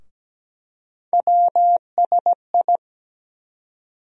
Morse code, heard as a keyed tone, about 17 wpm, 700 Hz. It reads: WSI
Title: USA Call Sign Postfixes - 3 Letters 17wpm